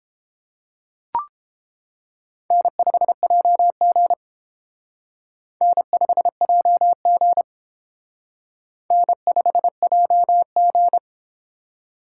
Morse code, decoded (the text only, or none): N5JG N5JG N5JG